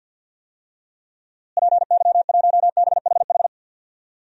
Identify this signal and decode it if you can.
Morse code, heard as a keyed tone, about 53 wpm, 700 Hz. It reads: PY1BHH
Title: DXCC-100 Call Signs - 6 Characters - Repeated 2x - 50wpm